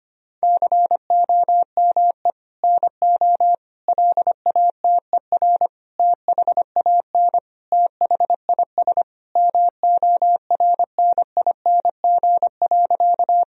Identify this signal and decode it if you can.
Morse code, heard as a keyed tone, about 25 wpm, 700 Hz. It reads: COME NO LATER THAN THIS MORNING.
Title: Sentences from Top 1000 Words - Review Part 4 - 25wpm